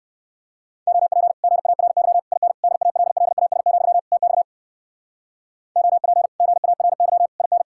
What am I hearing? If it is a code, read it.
Morse code, heard as a keyed tone, about 62 wpm, 700 Hz. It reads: OP GARY IN DALLAS, TX OP GARY IN